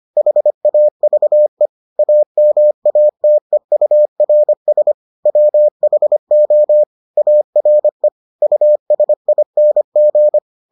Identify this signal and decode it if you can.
Morse code, heard as a keyed tone, about 25 wpm, 600 Hz. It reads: HAVE AMATEURS WHO ARE USING